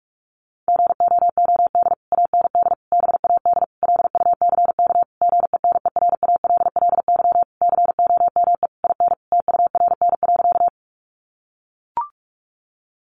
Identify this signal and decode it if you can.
Morse code, heard as a keyed tone, about 44 wpm, 700 Hz. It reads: GOOD AND BAD LUCK GENERALLY COME IN TURN.
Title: Sentences from Top 1500 Words 45wpm